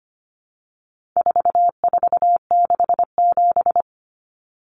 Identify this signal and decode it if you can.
Morse code, heard as a keyed tone, about 25 wpm, 700 Hz. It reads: 4467
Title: Numbers - 4-Digit 25wpm